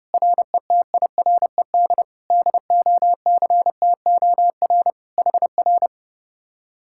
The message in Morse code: RETIRED DOCTOR HR

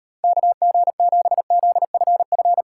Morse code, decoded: KG7ZFF

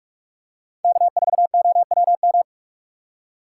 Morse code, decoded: K4OWM

